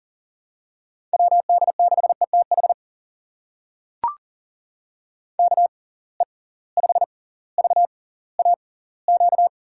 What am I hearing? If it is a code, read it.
Morse code, heard as a keyed tone, about 40 wpm, spaced as for 20 wpm, 700 Hz. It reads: WD6ETH KE5VAQ
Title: USA Call Signs - 6 Characters - 6x Character Spacing - 40wpm